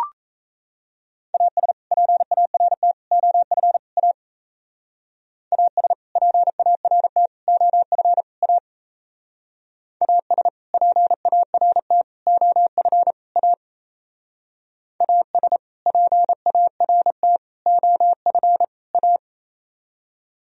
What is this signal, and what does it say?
Morse code, heard as a keyed tone, about 35 wpm, 700 Hz. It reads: AS PART OF A AS PART OF A AS PART OF A AS PART OF A